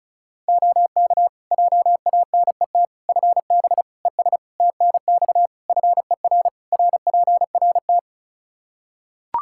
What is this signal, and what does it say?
Morse code, heard as a keyed tone, about 36 wpm, 700 Hz. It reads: OK JANET FB ES TNX FER RPRT
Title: Practice QSO - Exchange 2 - Info - 35wpm